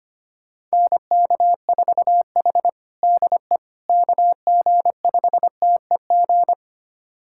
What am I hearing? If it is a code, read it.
Morse code, heard as a keyed tone, about 25 wpm, 700 Hz. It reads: NK4H DE KG5TEG
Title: USA Callsign DE Callsign 25wpm